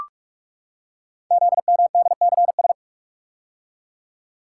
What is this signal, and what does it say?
Morse code, heard as a keyed tone, about 44 wpm, 700 Hz. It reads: ZMDCS
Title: Code Groups - 5 Letters - 45wpm